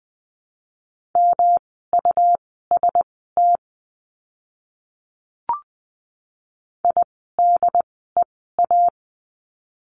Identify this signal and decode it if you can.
Morse code, heard as a keyed tone, about 20 wpm, spaced as for 14 wpm, 700 Hz. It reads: MUST IDEA
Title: 108 Lesson 18f - Letter m - Words 20wpm